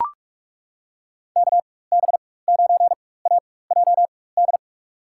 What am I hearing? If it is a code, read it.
Morse code, heard as a keyed tone, about 42 wpm, spaced as for 20 wpm, 700 Hz. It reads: KB9AJD
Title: USA Call Signs - 4x Character Spacing - 45wpm